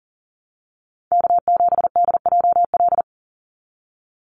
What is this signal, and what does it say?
Morse code, heard as a keyed tone, about 40 wpm, 700 Hz. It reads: K7DJL
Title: NA SOTA Chaser Callsigns 40wpm